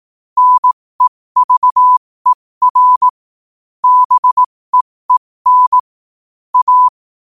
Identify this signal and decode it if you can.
Morse code, heard as a keyed tone, about 18 wpm, spaced as for 15 wpm, 1000 Hz. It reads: NEVER BEEN A
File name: qotdfiles / 15wpm